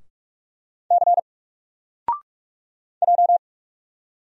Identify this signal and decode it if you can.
Morse code, heard as a keyed tone, about 44 wpm, 700 Hz. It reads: C J
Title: Single Letter-Number Alt 45wpm